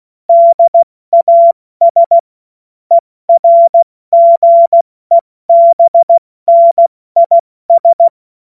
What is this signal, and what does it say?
Morse code, heard as a keyed tone, about 16 wpm, 680 Hz. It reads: DAS ERGEBNIS
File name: Morsekurs / Witze / Witz 21. Dezember 2021, 70 F80